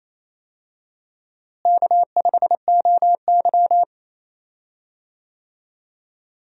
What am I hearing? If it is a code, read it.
Morse code, heard as a keyed tone, about 28 wpm, 700 Hz. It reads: K5OY
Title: USA Call Signs - 4 Characters - Repeated 2x - 28wpm